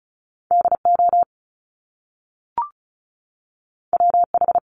Morse code, heard as a keyed tone, about 36 wpm, 700 Hz. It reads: DO WH
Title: Top Two Letter Combinations in Words 35wpm